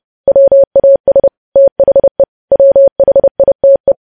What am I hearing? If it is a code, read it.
Morse code, heard as a keyed tone, about 30 wpm, 550 Hz. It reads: WAS THE WHITE
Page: news headlines in morse code at 30 wpm